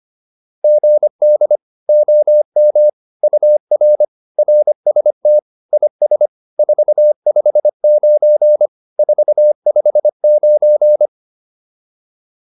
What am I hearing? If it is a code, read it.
Morse code, heard as a keyed tone, about 25 wpm, 600 Hz. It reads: GD OM UR RST IS 459 459
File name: QSO406 25